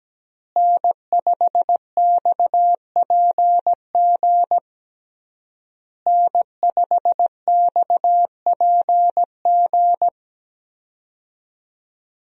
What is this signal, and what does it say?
Morse code, heard as a keyed tone, about 17 wpm, 700 Hz. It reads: N5XPG N5XPG
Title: USA Call Signs - 5 Characters - Repeated 3x - 17wpm